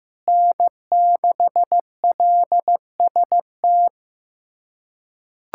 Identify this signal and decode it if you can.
Morse code, heard as a keyed tone, about 15 wpm, 700 Hz. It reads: N6LST
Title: USA Callsign DE Callsign 15wpm